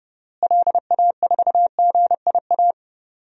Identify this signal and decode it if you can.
Morse code, heard as a keyed tone, about 30 wpm, 700 Hz. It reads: LA4GIA
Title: DXCC-100 Call Signs - 6 Characters - Repeated 2x - 30wpm